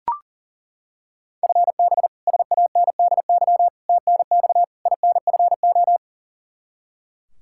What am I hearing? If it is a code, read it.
Morse code, heard as a keyed tone, about 40 wpm, 690 Hz. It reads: FB SANDY TNX INFO
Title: Practice QSO - Exchange 3 - More Info - 40wpm